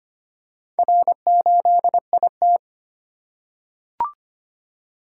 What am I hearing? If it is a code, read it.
Morse code, heard as a keyed tone, about 25 wpm, 700 Hz. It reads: R8IT
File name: practice / 25wpm / Four Letter-Number Combinations - 25wpm